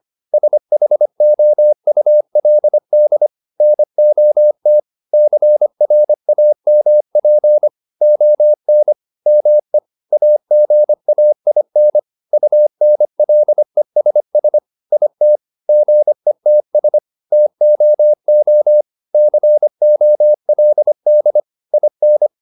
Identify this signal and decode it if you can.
Morse code, heard as a keyed tone, about 25 wpm, 600 Hz. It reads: SHOULD NOT CRAMP ON ME AGAIN UNLESS IT GETS TOO COLD IN